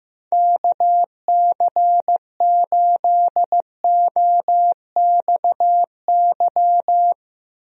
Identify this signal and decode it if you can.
Morse code, heard as a keyed tone, about 15 wpm, 700 Hz. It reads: KC8OXY
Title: USA Call Signs - 6 Characters - Repeated 3x - 15wpm